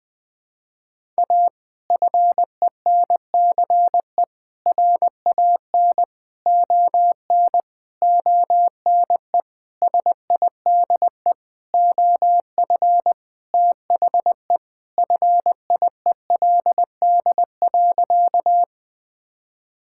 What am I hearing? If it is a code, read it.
Morse code, heard as a keyed tone, about 20 wpm, 700 Hz. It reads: A FENCE RAN ON ONE SIDE OF THE FIELD.
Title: Sentences with 5-Word Phrases - Spoken Beginning - 20wpm